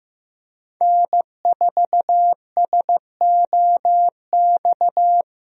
Morse code, heard as a keyed tone, about 15 wpm, 700 Hz. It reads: N4SOX
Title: USA Call Signs - 5 Characters - Repeated 3x - 15wpm